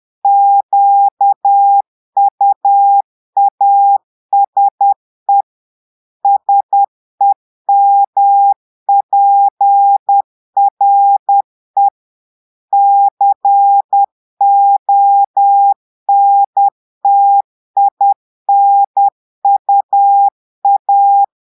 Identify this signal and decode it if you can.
Morse code, heard as a keyed tone, about 10 wpm, 800 Hz. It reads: QUASE SEMPRE CONTINUA
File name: Portugais - texte en clair 1 - vitesse - 10 mots minute